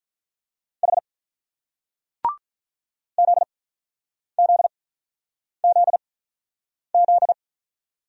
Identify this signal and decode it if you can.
Morse code, heard as a keyed tone, about 48 wpm, spaced as for 9 wpm, 700 Hz. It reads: H ZZZZ